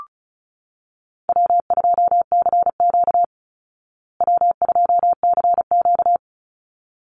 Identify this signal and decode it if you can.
Morse code, heard as a keyed tone, about 34 wpm, 700 Hz. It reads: W2CQ W2CQ
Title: Call Signs Repeated 3x - 35wpm